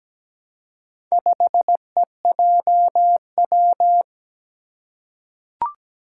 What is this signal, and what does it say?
Morse code, heard as a keyed tone, about 17 wpm, 700 Hz. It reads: 5EJW
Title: Four Challenging Letter-Number Combinations 17wpm